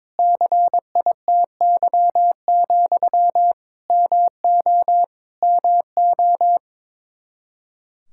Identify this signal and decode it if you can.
Morse code, heard as a keyed tone, about 21 wpm, 700 Hz. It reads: CITY, MO MO
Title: K1USN SST QSO Practice 22wpm